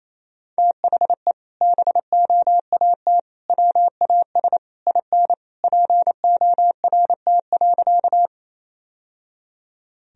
Morse code, heard as a keyed tone, about 28 wpm, 700 Hz. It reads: THE BOAT WAS IN PORT.
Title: Sentences from Top 1000 Words - Review Part 1 - 28wpm